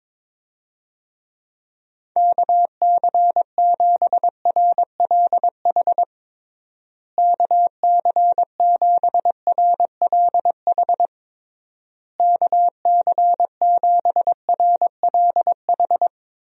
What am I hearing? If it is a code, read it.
Morse code, heard as a keyed tone, about 22 wpm, 700 Hz. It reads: KC7RLH KC7RLH KC7RLH
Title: USA Call Signs - 6 Characters - Repeated 3x - 22wpm